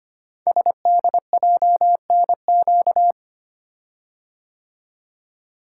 Morse code, heard as a keyed tone, about 25 wpm, 700 Hz. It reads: SDJNQ
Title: Code Groups - 5 Letter:Numbers - 25wpm